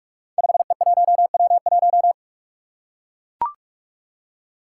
Morse code, heard as a keyed tone, about 44 wpm, 700 Hz. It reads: 5E1W1
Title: Five Challenging Letter-Number Combinations 45wpm